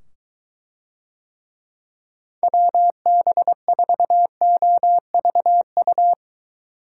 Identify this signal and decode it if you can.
Morse code, heard as a keyed tone, about 23 wpm, 700 Hz. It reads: WB4OVU